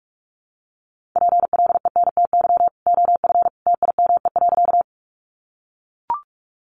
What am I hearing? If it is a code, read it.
Morse code, heard as a keyed tone, about 44 wpm, 700 Hz. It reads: PLENTY OF TIME.
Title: Sentences from Top 1300 Words 45wpm